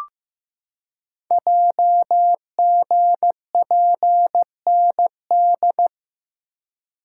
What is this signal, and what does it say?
Morse code, heard as a keyed tone, about 15 wpm, 700 Hz. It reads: JGPND
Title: Code Groups - 5 Letters - 15wpm